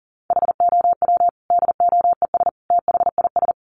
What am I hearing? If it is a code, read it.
Morse code, heard as a keyed tone, about 40 wpm, 700 Hz. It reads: HOW DOES THIS
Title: Sentences from Top 400 Words 40wpm